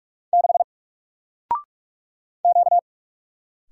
Morse code, heard as a keyed tone, about 46 wpm, 700 Hz. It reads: 6 Q
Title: Single Letter-Number 45wpm